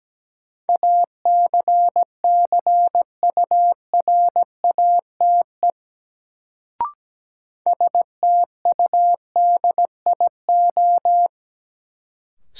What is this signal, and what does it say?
Morse code, heard as a keyed tone, about 17 wpm, 700 Hz. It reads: ACCURATE STUDIO
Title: Sentences from Top 1200 Words - New Word Drill - 17wpm